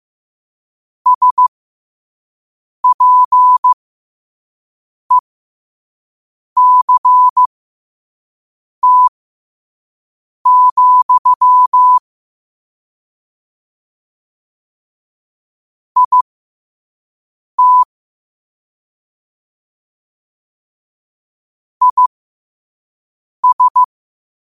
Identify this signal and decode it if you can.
Morse code, heard as a keyed tone, about 15 wpm, spaced as for 5 wpm, 1000 Hz. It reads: SPECT, IT IS
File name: qotdfiles / 5wpm